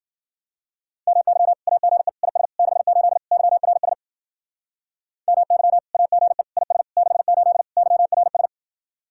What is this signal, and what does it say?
Morse code, heard as a keyed tone, about 56 wpm, 700 Hz. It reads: MY AGE IS 68 YRS MY AGE IS 68 YRS